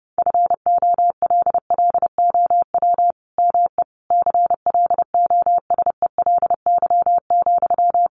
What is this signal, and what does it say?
Morse code, heard as a keyed tone, about 30 wpm, 700 Hz. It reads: FOLLOW ME CLOSELY,
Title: Sentences from Top 1400 Words 30wpm